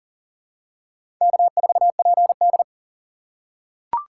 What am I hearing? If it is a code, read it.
Morse code, heard as a keyed tone, about 37 wpm, 700 Hz. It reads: K4PD
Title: USA Call Signs - 4 Characters - 6x Character Spacing - 40wpm